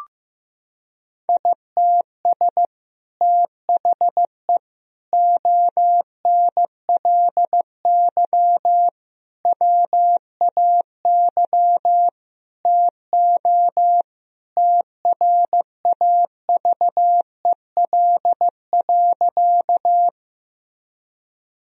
Morse code, heard as a keyed tone, about 15 wpm, 700 Hz. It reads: ITS THE ONLY WAY TO TRAVEL.
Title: Sentences from Top 1000 Words - Review Part 2 - 15wpm